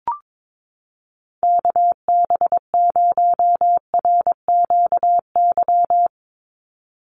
Morse code, heard as a keyed tone, about 22 wpm, 700 Hz. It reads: KB0RQY
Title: USA Call Signs 22wpm